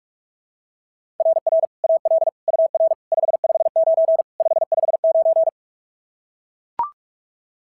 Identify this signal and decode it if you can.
Morse code, heard as a keyed tone, about 44 wpm, 690 Hz. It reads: RR AL UR 559 559
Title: SOTA QSO Practice 45wpm